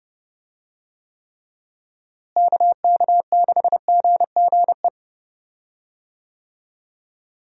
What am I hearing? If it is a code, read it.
Morse code, heard as a keyed tone, about 29 wpm, 700 Hz. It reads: KK6GGE
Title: USA Call Signs - 6 Characters - Repeated 2x - 30wpm